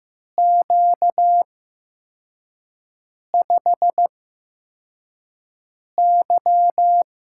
Morse code, heard as a keyed tone, about 15 wpm, spaced as for 4 wpm, 700 Hz. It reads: Q5Y
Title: USA Call Signs - 4 Characters - 8x Character Spacing - 15wpm